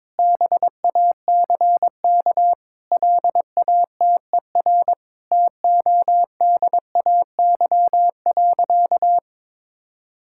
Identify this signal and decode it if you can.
Morse code, heard as a keyed tone, about 22 wpm, 700 Hz. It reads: BACK LATER TODAY.
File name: practice / 22wpm / Sentences with 4-Word Phrases - 22wpm